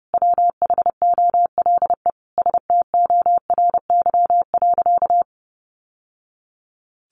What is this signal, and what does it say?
Morse code, heard as a keyed tone, about 28 wpm, 700 Hz. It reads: WHOLE STORY.
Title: Sentences from Top 1100 Words 30wpm